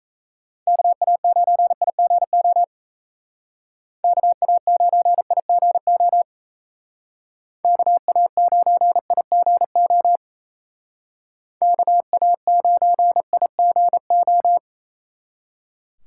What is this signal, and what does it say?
Morse code, heard as a keyed tone, about 35 wpm, 700 Hz. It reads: KA9IGO KA9IGO KA9IGO KA9IGO
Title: Speed Racing - USA 2x3 Call Signs 28wpm